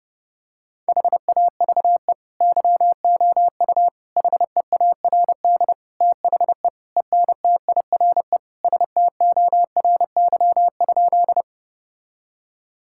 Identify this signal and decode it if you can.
Morse code, heard as a keyed tone, about 30 wpm, 700 Hz. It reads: HAVE YOU HEARD THE ENTIRE STORY?
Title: Sentences from Top 1000 Words - Review Part 5 - 30wpm